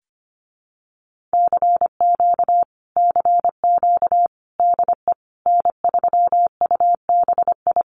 Morse code, heard as a keyed tone, about 25 wpm, 700 Hz. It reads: CQ CQ DE N3UBI
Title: Common QSO Phrases Encore 25wpm